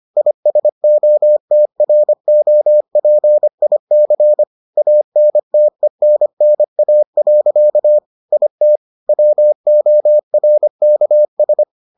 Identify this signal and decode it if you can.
Morse code, heard as a keyed tone, about 25 wpm, 600 Hz. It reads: ISOTROPIC ANTENNA. IT WORKS